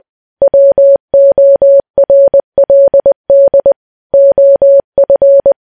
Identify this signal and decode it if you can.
Morse code, heard as a keyed tone, about 20 wpm, 560 Hz. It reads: WORLD OF
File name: morse